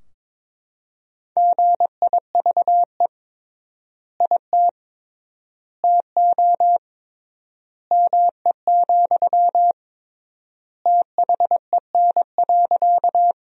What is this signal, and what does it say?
Morse code, heard as a keyed tone, about 22 wpm, 700 Hz. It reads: GIVE IT TO ME, THEN.
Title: 3x Word-Spacing - Sentences from Top 200 Words - 22wpm